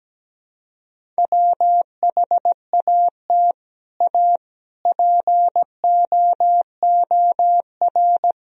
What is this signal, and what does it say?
Morse code, heard as a keyed tone, about 17 wpm, 700 Hz. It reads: WHAT A POOR
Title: Sentences from Top 1000 Words - Review Part 1 - 17wpm